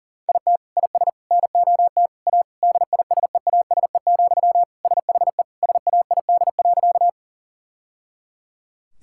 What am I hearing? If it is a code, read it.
Morse code, heard as a keyed tone, about 40 wpm, 700 Hz. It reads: IT IS NOT A DISEASE, SHE SAID.
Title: Sentences from Top 1000 Words - Review Part 6 - 40wpm